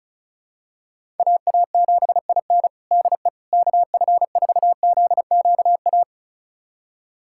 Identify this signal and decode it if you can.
Morse code, heard as a keyed tone, about 36 wpm, 700 Hz. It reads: AA7IN DE KF4ZQA
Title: USA Callsign DE Callsign 35wpm